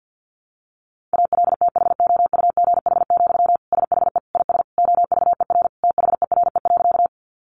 Morse code, heard as a keyed tone, about 50 wpm, 700 Hz. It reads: ALTHOUGH, SHE IS OVER THERE.
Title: Sentences from Top 600 Words 50wpm